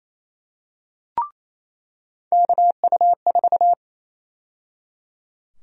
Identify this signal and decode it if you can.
Morse code, heard as a keyed tone, about 28 wpm, 700 Hz. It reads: KU4
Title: USA Call Sign Prefixes - 28wpm